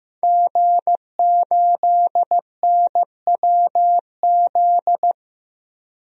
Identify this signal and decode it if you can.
Morse code, heard as a keyed tone, about 15 wpm, 700 Hz. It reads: G8NWZ
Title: DXCC-100 Call Signs - 5 Characters - 15wpm